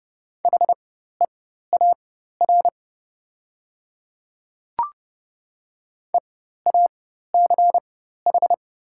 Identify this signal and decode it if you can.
Morse code, heard as a keyed tone, about 30 wpm, spaced as for 14 wpm, 690 Hz. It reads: HEAR EACH
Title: 4x Character Spacing - Top 500 Words - 30wpm